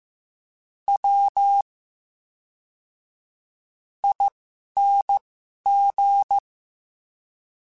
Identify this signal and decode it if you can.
Morse code, heard as a keyed tone, about 15 wpm, spaced as for 11 wpm, 800 Hz. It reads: W ING
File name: Special INGs / Special INGs-mp3s / VerySlowToFaster Ing @10